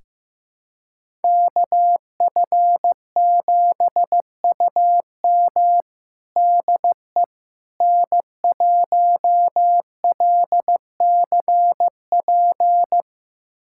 Morse code, heard as a keyed tone, about 15 wpm, 700 Hz. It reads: KF7UM DE N1LCP
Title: USA Callsign DE Callsign - 5 Characters - 15wpm